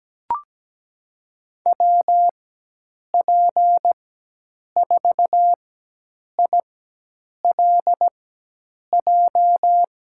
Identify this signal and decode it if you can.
Morse code, heard as a keyed tone, about 17 wpm, spaced as for 8 wpm, 700 Hz. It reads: WP4ILJ
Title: USA Call Signs - 6 Characters - 4x Character Spacing - 17wpm